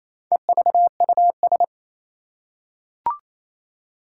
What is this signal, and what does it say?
Morse code, heard as a keyed tone, about 28 wpm, 700 Hz. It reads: EVUS
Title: Four Challenging Letter-Number Combinations 28wpm